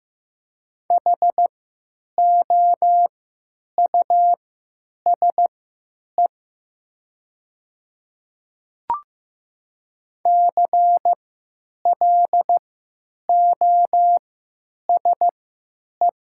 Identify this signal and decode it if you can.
Morse code, heard as a keyed tone, about 15 wpm, spaced as for 9 wpm, 700 Hz. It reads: HOUSE CLOSE
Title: 3x Character Spacing - Top 300 Words - 15wpm